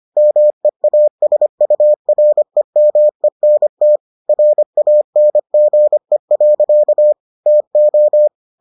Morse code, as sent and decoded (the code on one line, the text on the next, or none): -- . .- ... ..- .-. -- . -. - / .-. .- -. --. .-.-.- / - ---
MEASUREMENT RANGE. TO